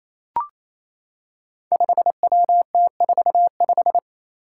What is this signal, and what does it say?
Morse code, heard as a keyed tone, about 28 wpm, 700 Hz. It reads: 5WT45
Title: Five Challenging Letter-Number Combinations 28wpm